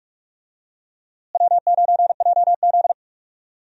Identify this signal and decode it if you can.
Morse code, heard as a keyed tone, about 46 wpm, 700 Hz. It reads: W9JZ
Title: USA Call Signs - 4 Characters - Repeated 2x - 45wpm